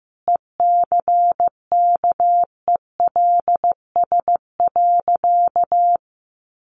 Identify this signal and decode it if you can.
Morse code, heard as a keyed tone, about 15 wpm, 700 Hz. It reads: ECKELS.
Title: A Sound Of Thunder 15wpm